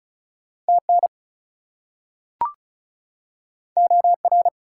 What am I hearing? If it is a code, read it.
Morse code, heard as a keyed tone, about 36 wpm, 700 Hz. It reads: TN OR
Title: US States and Canadian Provinces 35wpm